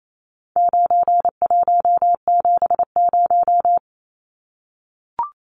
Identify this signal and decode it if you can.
Morse code, heard as a keyed tone, about 28 wpm, 700 Hz. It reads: 9170
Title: Numbers - 4-Digit 28wpm